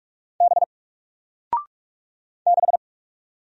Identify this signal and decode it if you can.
Morse code, heard as a keyed tone, about 44 wpm, 700 Hz. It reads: B 6
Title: Single Character (Letter:Number:Punctuation:Prosign) 45wpm